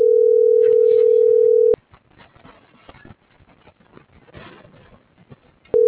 Ringback tone (North America), 440 + 480 Hz.